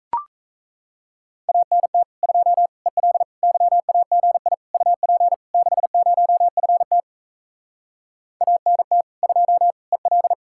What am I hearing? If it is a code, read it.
Morse code, heard as a keyed tone, about 38 wpm, 700 Hz. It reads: ANT 2 EL YAGI UP 60FT ANT 2 EL